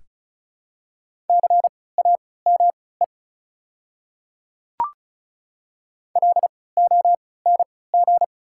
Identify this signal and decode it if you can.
Morse code, heard as a keyed tone, about 36 wpm, spaced as for 20 wpm, 700 Hz. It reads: CAME LONG
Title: 3x Character Spacing - Top 300 Words - 35wpm